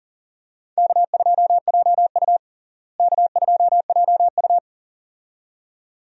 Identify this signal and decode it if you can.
Morse code, heard as a keyed tone, about 40 wpm, 700 Hz. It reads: K2JU K2JU
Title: USA Call Signs - 4 Characters - Repeated 2x - 40wpm